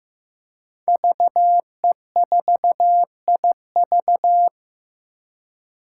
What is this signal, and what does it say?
Morse code, heard as a keyed tone, about 15 wpm, 700 Hz. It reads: VE4IV
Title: Five Challenging Letter-Number Combinations 15wpm